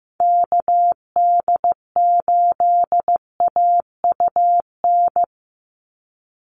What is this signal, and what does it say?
Morse code, heard as a keyed tone, about 15 wpm, 700 Hz. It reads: KD8AUN
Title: USA Call Signs 15wpm